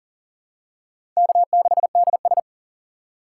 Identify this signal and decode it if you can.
Morse code, heard as a keyed tone, about 38 wpm, 700 Hz. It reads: K6DS
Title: USA Call Signs - 4 Characters - 3x Character Spacing - 40wpm